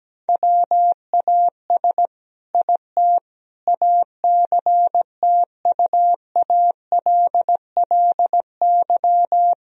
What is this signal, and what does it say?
Morse code, heard as a keyed tone, about 17 wpm, 700 Hz. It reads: WAS IT ACTUALLY
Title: Sentences from Top 1000 Words - Review Part 4 - 17wpm